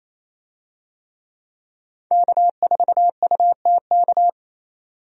Morse code, heard as a keyed tone, about 28 wpm, 700 Hz. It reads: K4UTK
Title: USA Call Signs - 5 Characters - 28wpm